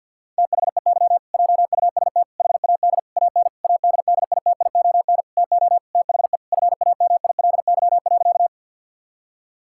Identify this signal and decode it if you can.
Morse code, heard as a keyed tone, about 49 wpm, 700 Hz. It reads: THEY JUST HAD AN ADDITION TO THE FAMILY.
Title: Sentences from Top 1000 Words - Review Part 6 - 50wpm